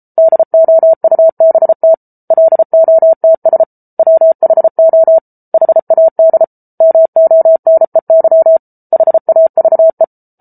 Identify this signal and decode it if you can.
Morse code, heard as a keyed tone, about 34 wpm, 650 Hz. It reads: DOUBT LOTS WHO HAD MONEY HAVE
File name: Book Two / 07-Chapter 7 - 33 WPM